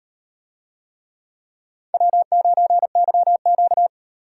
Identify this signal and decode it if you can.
Morse code, heard as a keyed tone, about 37 wpm, 700 Hz. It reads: W9YQ